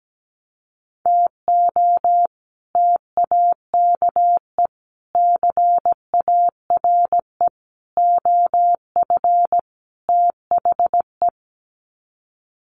Morse code, heard as a keyed tone, about 17 wpm, 700 Hz. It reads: TO TAKE CARE OF THE
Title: Sets of 5 Words Encore 17wpm